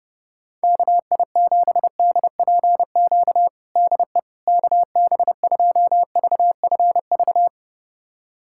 Morse code, heard as a keyed tone, about 30 wpm, 700 Hz. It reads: KI7DPQ DE KB2VFV
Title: USA Callsign DE Callsign - 6 Characters - 30wpm